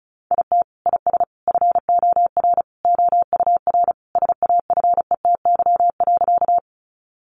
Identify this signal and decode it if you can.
Morse code, heard as a keyed tone, about 34 wpm, 700 Hz. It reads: IT IS FOR OUR SAFETY.
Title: Sentences from Top 700 Words 35wpm